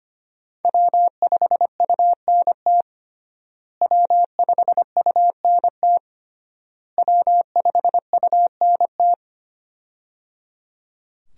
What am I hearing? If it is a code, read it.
Morse code, heard as a keyed tone, about 25 wpm, 700 Hz. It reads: W5UNT W5UNT W5UNT
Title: USA Call Signs - 5 Characters - Repeated 3x - 25wpm